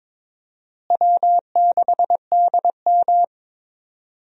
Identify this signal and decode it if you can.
Morse code, heard as a keyed tone, about 22 wpm, 700 Hz. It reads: W6DM
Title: USA Call Signs - 4 Characters - Repeated 3x - 22wpm